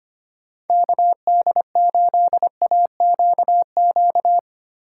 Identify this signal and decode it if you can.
Morse code, heard as a keyed tone, about 25 wpm, 700 Hz. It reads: KD8AQQ
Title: USA Call Signs - 6 Characters - 3x Character Spacing - 25wpm